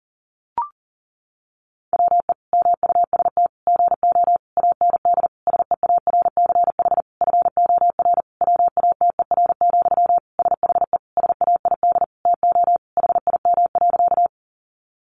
Morse code, heard as a keyed tone, about 40 wpm, 700 Hz. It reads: WE MUST GO AND SEARCH FOR WATER, SHE SAID TO HIM.
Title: Sentences from Top 800 Words 40wpm